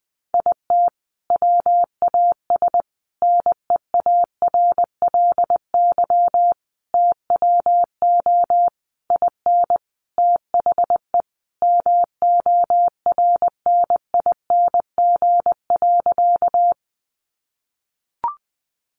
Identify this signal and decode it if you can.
Morse code, heard as a keyed tone, about 20 wpm, 700 Hz. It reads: IT WAS NEARLY TWO IN THE MORNING.
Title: Sentences from Top 1300 Words 20wpm